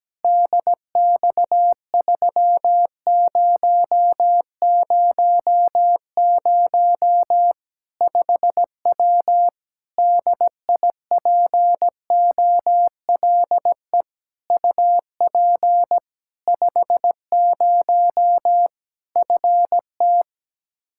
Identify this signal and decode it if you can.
Morse code, heard as a keyed tone, about 17 wpm, 700 Hz. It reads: DX3000 5W DIPOLE UP 50 FT
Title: Practice QSO - Exchange 2 - Info - 17wpm